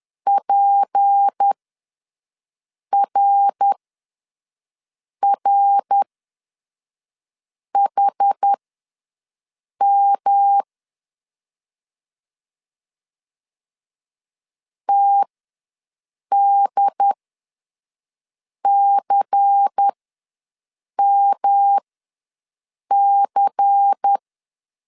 Morse code, heard as a keyed tone, about 10 wpm, spaced as for 5 wpm, 800 Hz. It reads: PRRHM TDCMC
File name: LEZIONI / 6 PKX0 / 6 PKX0 ws 10wpm 4220